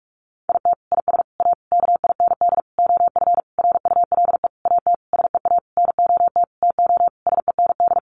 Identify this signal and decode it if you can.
Morse code, heard as a keyed tone, about 44 wpm, 700 Hz. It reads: IT IS A KIND OF RULE AT SEA NOT TO SEND
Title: Sentences from Top 1500 Words 45wpm